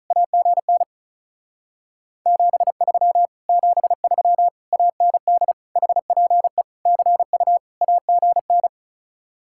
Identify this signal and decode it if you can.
Morse code, heard as a keyed tone, about 46 wpm, 700 Hz. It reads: AGN 73 73 AND HPE CU AGN